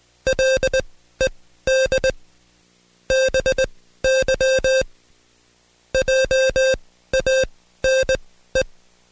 Morse code, heard as a keyed tone, about 20 wpm, spaced as for 14 wpm, 530 Hz. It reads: LED BY JANE
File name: Code Quick / AudioDownld2 / Seg133-Stepper6 13WPM